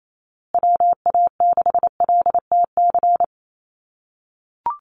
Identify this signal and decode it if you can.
Morse code, heard as a keyed tone, about 28 wpm, 700 Hz. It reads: WA6LTC
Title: USA Call Signs Repeated 3x 28wpm